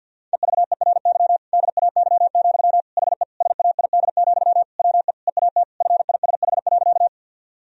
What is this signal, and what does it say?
Morse code, heard as a keyed tone, about 50 wpm, 700 Hz. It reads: EVERY DAY, HE SAID, WE EAT FISH.
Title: Sentences from Top 1000 Words - Review Part 1 - 50wpm